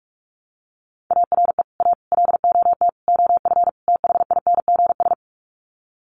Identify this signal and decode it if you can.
Morse code, heard as a keyed tone, about 44 wpm, 700 Hz. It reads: ARE A LOT OF THINGS
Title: Sets of 5 Words Encore2 45wpm